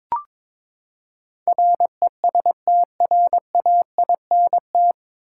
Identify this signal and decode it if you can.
Morse code, heard as a keyed tone, about 21 wpm, 700 Hz. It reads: RESTRAINT
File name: practice / 22wpm / Nine Letter Words - 22wpm